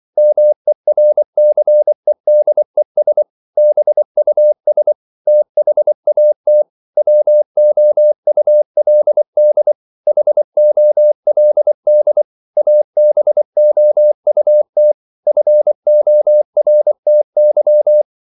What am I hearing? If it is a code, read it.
Morse code, heard as a keyed tone, about 24 wpm, 600 Hz. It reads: MERCEDES BUS THAT WOULD HOLD ABOUT FORTY